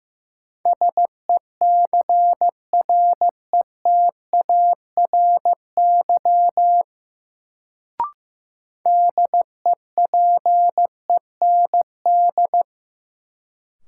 Morse code, heard as a keyed tone, about 15 wpm, 700 Hz. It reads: SECRETARY DEPEND
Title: Sentences from Top 1300 Words - New Word Drill - 15wpm